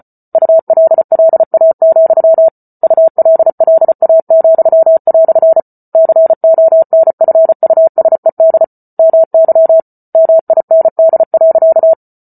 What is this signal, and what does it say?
Morse code, heard as a keyed tone, about 34 wpm, 650 Hz. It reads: ULLA, ULLA," CONFUSED MY MIND.